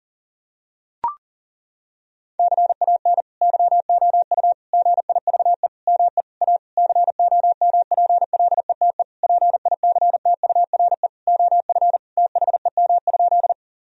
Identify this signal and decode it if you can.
Morse code, heard as a keyed tone, about 40 wpm, 700 Hz. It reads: CAN YOU GIVE ME A COMPLETE PICTURE OF THEM?
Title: Sentences from Top 1000 Words - Review Part 2 - 40wpm